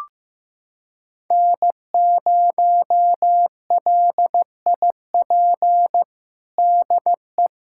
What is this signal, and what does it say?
Morse code, heard as a keyed tone, about 15 wpm, 700 Hz. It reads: N0LIP DE
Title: USA Callsign DE Callsign - 5 Characters - 15wpm